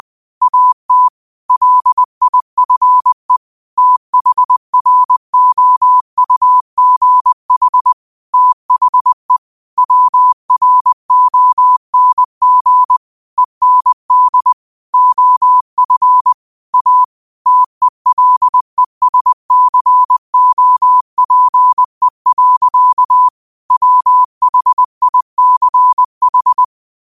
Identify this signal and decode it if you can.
Morse code, heard as a keyed tone, about 20 wpm, 1000 Hz. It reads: AT LIFE THROUGH THE WRONG END OF A TELESCOPE. WHICH
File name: qotdfiles / 20wpm